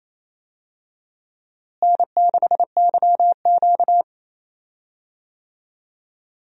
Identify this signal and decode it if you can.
Morse code, heard as a keyed tone, about 28 wpm, 700 Hz. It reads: N6YQ
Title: USA Call Signs - 4 Characters - Repeated 3x - 28wpm